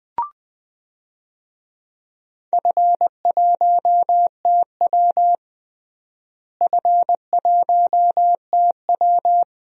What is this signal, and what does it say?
Morse code, heard as a keyed tone, about 20 wpm, 700 Hz. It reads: F1TW F1TW
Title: DXCC-100 Call Signs - 4 Characters - Repeated 2x - 20wpm